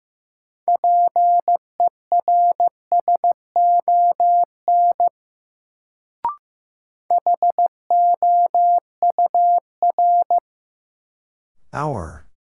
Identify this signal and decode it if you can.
Morse code, heard as a keyed tone, about 15 wpm, 700 Hz. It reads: PERSON HOUR
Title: Sentences from Top 400 Words - New Word Drill - 15wpm